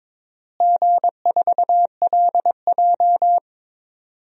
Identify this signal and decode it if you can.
Morse code, heard as a keyed tone, about 22 wpm, 700 Hz. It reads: G4LJ
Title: DXCC-100 Call Signs - 4 Characters - Repeated 3x - 22wpm